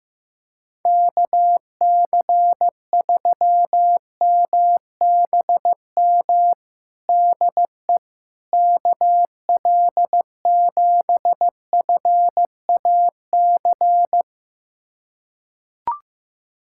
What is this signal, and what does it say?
Morse code, heard as a keyed tone, about 14 wpm, 700 Hz. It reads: KC3MBM DE KL7FAC
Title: USA Callsign DE Callsign 15wpm